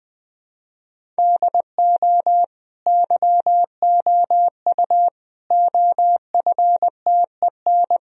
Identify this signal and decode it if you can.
Morse code, heard as a keyed tone, about 19 wpm, 700 Hz. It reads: DO YOU OFTEN
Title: Sentences from Top 1000 Words - Review Part 1 - 20wpm